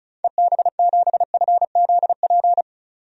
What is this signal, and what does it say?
Morse code, heard as a keyed tone, about 36 wpm, 700 Hz. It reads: EB7FZP
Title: DXCC-100 Call Signs - 6 Characters - 35wpm